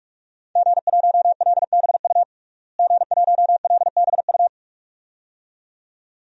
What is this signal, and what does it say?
Morse code, heard as a keyed tone, about 46 wpm, 700 Hz. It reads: G1LBU G1LBU
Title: DXCC-100 Call Signs - 5 Characters - Repeated 2x - 45wpm